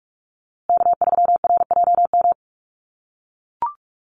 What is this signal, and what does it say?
Morse code, heard as a keyed tone, about 44 wpm, 700 Hz. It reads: K3RJM
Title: USA Call Signs Repeated 2x 45wpm